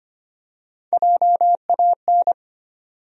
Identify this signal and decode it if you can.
Morse code, heard as a keyed tone, about 25 wpm, 700 Hz. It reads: JAN
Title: USA Common Names - Max 3 Letters - 25wpm